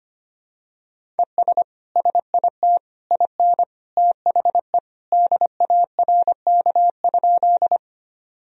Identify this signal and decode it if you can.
Morse code, heard as a keyed tone, about 25 wpm, 700 Hz. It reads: ES SIT IN THE DARK?
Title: Sentences from Top 500 Words and Abbreviations 25wpm